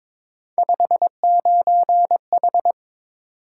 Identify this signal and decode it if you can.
Morse code, heard as a keyed tone, about 22 wpm, 700 Hz. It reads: 59H
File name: practice / 22wpm / Three Letter-Number Combinations - 22wpm